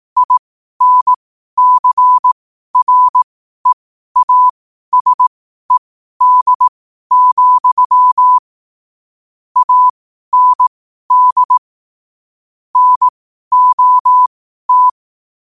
Morse code, heard as a keyed tone, about 18 wpm, spaced as for 12 wpm, 1000 Hz. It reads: INCREASED, AND NOT
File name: qotdfiles / 12wpm